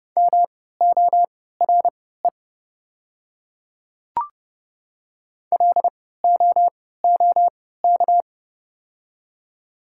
Morse code, heard as a keyed tone, about 30 wpm, spaced as for 17 wpm, 700 Hz. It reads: MORE LOOK
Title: 3x Character Spacing - Top 100 Words - 30wpm